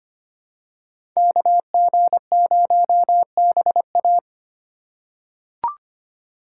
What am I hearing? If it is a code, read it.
Morse code, heard as a keyed tone, about 25 wpm, 700 Hz. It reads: KG0BA
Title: USA Call Signs - 5 Characters - 2x Character Spacing - 25wpm